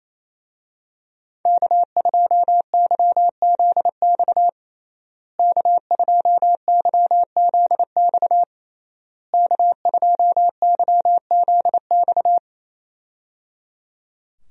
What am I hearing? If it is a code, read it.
Morse code, heard as a keyed tone, about 28 wpm, 700 Hz. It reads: K2YZX K2YZX K2YZX
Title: USA Call Signs - 5 Characters - Repeated 3x - 28wpm